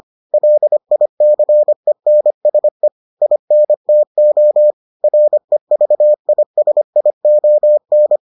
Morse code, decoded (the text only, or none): LICENSE INTO REVISION